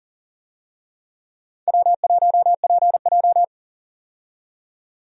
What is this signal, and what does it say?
Morse code, heard as a keyed tone, about 39 wpm, 700 Hz. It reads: W1PJ
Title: USA Call Signs - 4 Characters - 40wpm